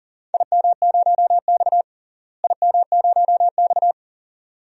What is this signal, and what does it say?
Morse code, heard as a keyed tone, about 40 wpm, 700 Hz. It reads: IM0X IM0X
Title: DXCC-100 Call Signs - 4 Characters - Repeated 2x - 40wpm